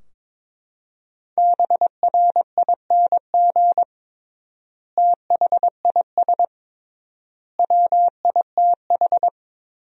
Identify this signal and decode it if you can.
Morse code, heard as a keyed tone, about 22 wpm, 700 Hz. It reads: BRING THIS WITH
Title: 3x Word-Spacing - Sentences from Top 500 Words - 22wpm